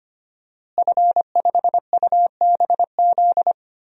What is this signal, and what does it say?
Morse code, heard as a keyed tone, about 25 wpm, 700 Hz. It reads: F5UBZ
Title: DXCC-100 Call Signs - 5 Characters - Repeated 3x - 25wpm